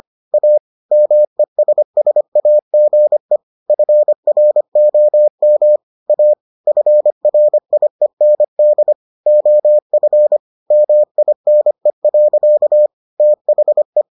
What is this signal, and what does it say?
Morse code, heard as a keyed tone, about 25 wpm, 600 Hz. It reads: A MESSAGE FROM A FRIEND OF MINE. THE